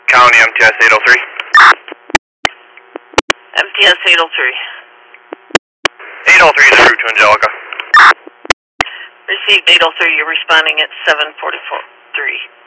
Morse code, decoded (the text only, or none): none